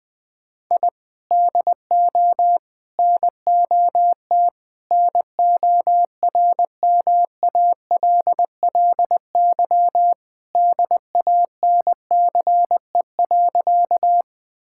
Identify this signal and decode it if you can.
Morse code, heard as a keyed tone, about 20 wpm, 700 Hz. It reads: I DO NOT NORMALLY DANCE.
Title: Sentences from Top 1000 Words - Review Part 6 - 20wpm